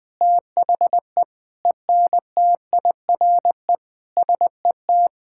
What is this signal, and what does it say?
Morse code, heard as a keyed tone, about 20 wpm, 700 Hz. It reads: THE ENTIRE SET
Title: Sentences from Top 1000 Words - Review Part 5 - 20wpm